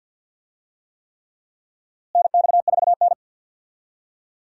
Morse code, heard as a keyed tone, about 51 wpm, 700 Hz. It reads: NX4N
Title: USA Call Signs - 4 Characters - Repeated 3x - 50wpm